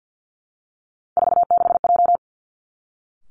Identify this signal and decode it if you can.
Morse code, heard as a keyed tone, about 50 wpm, 700 Hz. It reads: 46J
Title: Three Challenging Letter-Number Combinations 50wpm